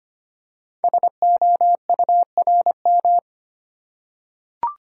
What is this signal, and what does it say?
Morse code, heard as a keyed tone, about 25 wpm, 700 Hz. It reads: SOURM
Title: Code Groups - 5 Letters - 25wpm